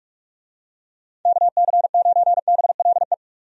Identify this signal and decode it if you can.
Morse code, heard as a keyed tone, about 42 wpm, 700 Hz. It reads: KC9BLE
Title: USA Call Signs - 6 Characters - 45wpm